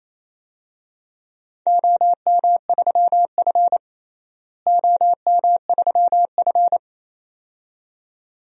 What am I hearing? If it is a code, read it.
Morse code, heard as a keyed tone, about 28 wpm, 700 Hz. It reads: OM3F OM3F
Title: DXCC-100 Call Signs - 4 Characters - Repeated 2x - 28wpm